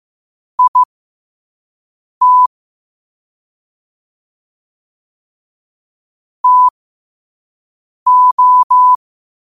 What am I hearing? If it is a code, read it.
Morse code, heard as a keyed tone, about 15 wpm, spaced as for 5 wpm, 1000 Hz. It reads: IT TO
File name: qotdfiles / 5wpm